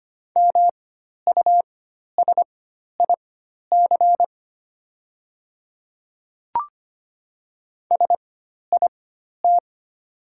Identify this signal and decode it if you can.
Morse code, heard as a keyed tone, about 25 wpm, spaced as for 12 wpm, 690 Hz. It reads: MUSIC SIT
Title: 4x Character Spacing - Top 500 Words - 25wpm